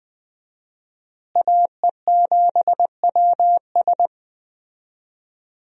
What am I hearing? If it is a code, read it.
Morse code, heard as a keyed tone, about 20 wpm, 700 Hz. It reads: AE7WS
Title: USA Call Signs - 5 Characters - Repeated 2x - 20wpm